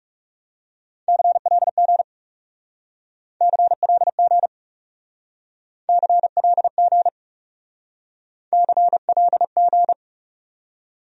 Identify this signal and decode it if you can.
Morse code, heard as a keyed tone, about 37 wpm, 700 Hz. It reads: CLG CLG CLG CLG
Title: Speed-Racing Abbreviations - 30wpm